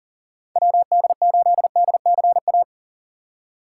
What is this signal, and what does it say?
Morse code, heard as a keyed tone, about 39 wpm, 700 Hz. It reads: WD8DCA
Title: USA Call Signs - 6 Characters - 2x Character Spacing - 40wpm